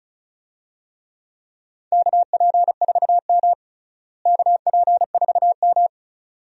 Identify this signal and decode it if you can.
Morse code, heard as a keyed tone, about 36 wpm, 700 Hz. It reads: KP4M KP4M
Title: DXCC-100 Call Signs - 4 Characters - Repeated 2x - 35wpm